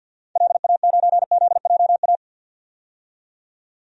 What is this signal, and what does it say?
Morse code, heard as a keyed tone, about 50 wpm, 700 Hz. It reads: LA9ZJA
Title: DXCC-100 Call Signs - 6 Characters - 50wpm